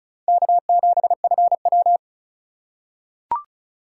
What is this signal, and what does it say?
Morse code, heard as a keyed tone, about 36 wpm, 690 Hz. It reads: K7FW
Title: USA Call Signs - 4 Characters - 3x Character Spacing - 35wpm